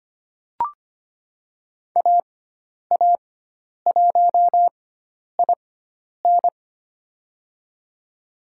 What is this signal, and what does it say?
Morse code, heard as a keyed tone, about 25 wpm, spaced as for 10 wpm, 700 Hz. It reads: AA1IN
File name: Lesson 16g - Letter c - Call Signs 25wpm